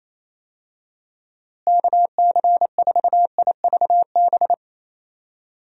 Morse code, heard as a keyed tone, about 28 wpm, 700 Hz. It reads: KC4IVB
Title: USA Call Signs - 6 Characters - Repeated 3x - 28wpm